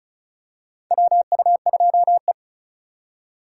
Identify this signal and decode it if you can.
Morse code, heard as a keyed tone, about 36 wpm, 700 Hz. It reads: WU2E
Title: USA Call Signs - 4 Characters - 2x Character Spacing - 35wpm